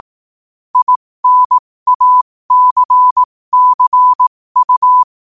Morse code, heard as a keyed tone, about 18 wpm, spaced as for 15 wpm, 1000 Hz. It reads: INACCU
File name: qotdfiles / 15wpm